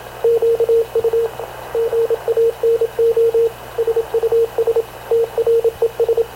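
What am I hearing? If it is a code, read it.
Morse code, heard as a keyed tone, about 26 wpm, 460 Hz, with radio noise. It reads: QU GANO SUS TRES
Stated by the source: Havana Radio / CLA sending Spanish press (PX) in Morse, recorded by Radio Officer Paul Dumesnil onboard CNAV QUEST/CZDO.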